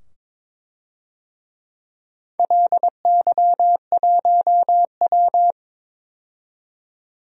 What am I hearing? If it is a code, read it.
Morse code, heard as a keyed tone, about 22 wpm, 700 Hz. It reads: LY1W
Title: DXCC-100 Call Signs - 4 Characters - 22wpm